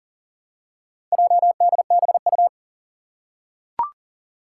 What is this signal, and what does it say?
Morse code, heard as a keyed tone, about 40 wpm, 700 Hz. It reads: JDBU
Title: Four Challenging Letter-Number Combinations 40wpm